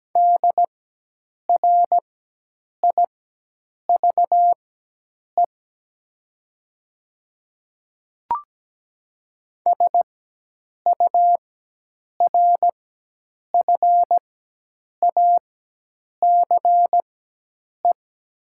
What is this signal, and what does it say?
Morse code, heard as a keyed tone, about 17 wpm, spaced as for 8 wpm, 700 Hz. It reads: DRIVE SURFACE
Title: 4x Character Spacing - Top 500 Words - 17wpm